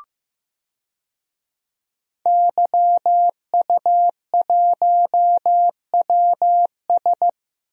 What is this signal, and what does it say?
Morse code, heard as a keyed tone, about 15 wpm, 700 Hz. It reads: YU1WS
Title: DXCC-100 Call Signs - 5 Characters - Repeated 2x - 15wpm